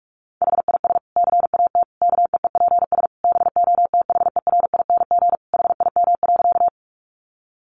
Morse code, heard as a keyed tone, about 44 wpm, 700 Hz. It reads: HIS CAT KEEPS BOTHERING HIM.
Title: Additional Easy Phrases 45wpm